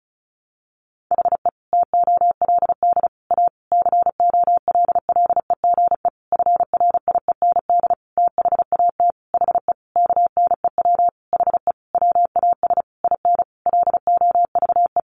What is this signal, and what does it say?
Morse code, heard as a keyed tone, about 34 wpm, 700 Hz. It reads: HE TOLD A COLLEGE FRIEND THAT HE KNEW HE WAS IN LOVE
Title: Sentences from Top 1300 Words 35wpm